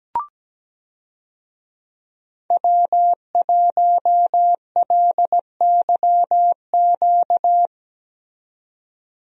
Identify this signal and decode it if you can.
Morse code, heard as a keyed tone, about 17 wpm, 700 Hz. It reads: W1LYQ
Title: USA Call Signs - 5 Characters - 17wpm